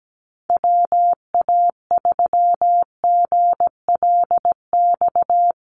Morse code, heard as a keyed tone, about 17 wpm, 700 Hz. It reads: WA3GLX
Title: USA Call Signs Repeated 2x 17wpm